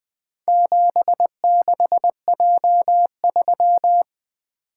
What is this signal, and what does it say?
Morse code, heard as a keyed tone, about 20 wpm, 700 Hz. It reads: 76J3
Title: Four Letter-Number Combinations - 20wpm